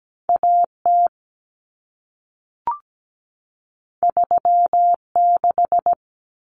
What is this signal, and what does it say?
Morse code, heard as a keyed tone, about 17 wpm, 700 Hz. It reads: AT 36
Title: Common Cut Numbers - 2-Digit 17wpm